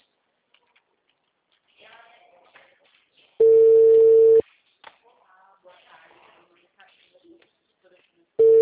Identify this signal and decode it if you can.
Ringback tone (China, 450 Hz), with speech.